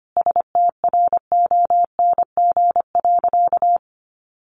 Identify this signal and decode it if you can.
Morse code, heard as a keyed tone, about 25 wpm, 700 Hz. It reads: STRONG.
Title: Sentences from Top 1400 Words 25wpm